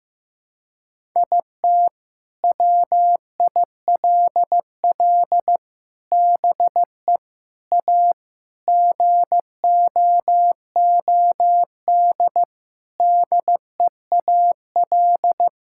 Morse code, heard as a keyed tone, about 15 wpm, 700 Hz. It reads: IT WILL BE A GOOD DEAL
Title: Sentences from Top 1000 Words - Review Part 6 - 15wpm